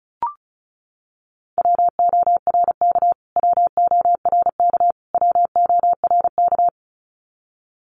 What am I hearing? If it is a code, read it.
Morse code, heard as a keyed tone, about 34 wpm, 700 Hz. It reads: WORK WORK WORK
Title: Top 200 Words Repeated 35wpm